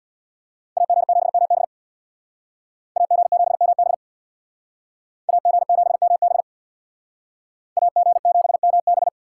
Morse code, heard as a keyed tone, about 62 wpm, 700 Hz. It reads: AG7MB AG7MB AG7MB AG7MB
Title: Speed Racing - USA 2x2 Call Signs 50wpm